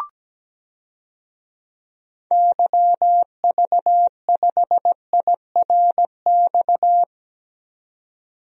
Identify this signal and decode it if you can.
Morse code, heard as a keyed tone, about 17 wpm, 700 Hz. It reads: YV5IRX
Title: DXCC-100 Call Signs - 6 Characters - 17wpm